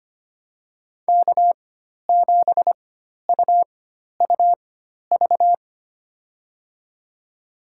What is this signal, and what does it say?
Morse code, heard as a keyed tone, about 25 wpm, spaced as for 12 wpm, 700 Hz. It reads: K7UUV
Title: USA Call Signs - 5 Characters - 4x Character Spacing - 25wpm